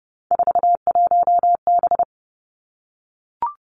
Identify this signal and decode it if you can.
Morse code, heard as a keyed tone, about 30 wpm, 700 Hz. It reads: 41B
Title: Three Challenging Letter-Number Combinations 30wpm